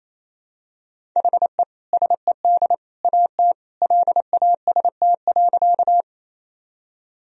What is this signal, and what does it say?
Morse code, heard as a keyed tone, about 28 wpm, 700 Hz. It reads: HE SED AT LAST.
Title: Sentences from Top 300 Words and Abbreviations 28wpm